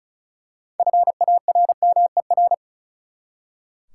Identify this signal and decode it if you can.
Morse code, heard as a keyed tone, about 36 wpm, 700 Hz. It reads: FARMER
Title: Six Letter Words - 35wpm